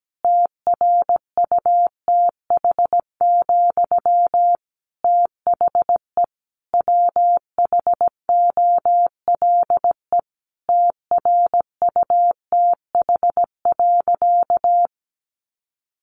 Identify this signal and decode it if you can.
Morse code, heard as a keyed tone, about 17 wpm, 700 Hz. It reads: TRUTH, THE WHOLE TRUTH.
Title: Sentences from Top 800 Words 17wpm